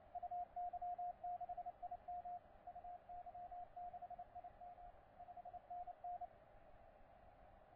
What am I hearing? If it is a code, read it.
Morse code, heard as a keyed tone, about 28 wpm, 710 Hz. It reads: UY6IM UY6IM 5NN
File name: C82DX 12CW